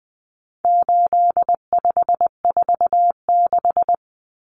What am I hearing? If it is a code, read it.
Morse code, heard as a keyed tone, about 20 wpm, 700 Hz. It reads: 8546
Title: Common Cut Numbers - 4-Digit 20wpm